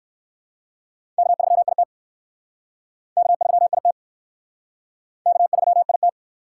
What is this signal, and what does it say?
Morse code, heard as a keyed tone, about 60 wpm, 700 Hz. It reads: K3IT K3IT K3IT